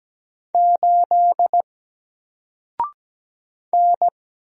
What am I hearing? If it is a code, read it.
Morse code, heard as a keyed tone, about 17 wpm, 700 Hz. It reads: 8 N
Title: Single Character (Letter:Number:Punctuation:Prosign) 17wpm